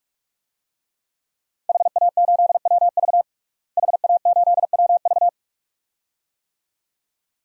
Morse code, heard as a keyed tone, about 46 wpm, 700 Hz. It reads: HA8WV HA8WV
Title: DXCC-100 Call Signs - 5 Characters - Repeated 2x - 45wpm